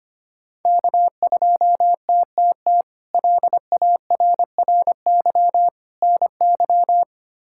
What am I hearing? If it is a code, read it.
Morse code, heard as a keyed tone, about 25 wpm, 700 Hz. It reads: K2TTT LARRY NY
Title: K1USN SST QSO Practice - 25wpm